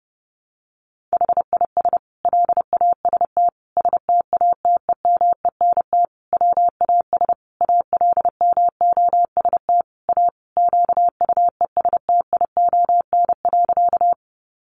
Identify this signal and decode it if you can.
Morse code, heard as a keyed tone, about 30 wpm, 700 Hz. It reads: HIS LAST STATEMENT WAS ALMOST A QUESTION.
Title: Sentences from Top 1200 Words 30wpm